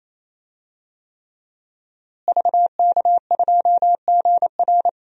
Morse code, heard as a keyed tone, about 28 wpm, 700 Hz. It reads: VK2GR
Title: DXCC-100 Call Signs - 5 Characters - Repeated 3x - 28wpm